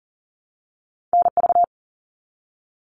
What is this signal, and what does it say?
Morse code, heard as a keyed tone, about 39 wpm, 700 Hz. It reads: NV
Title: All Cut Numbers - 2-Digit 40wpm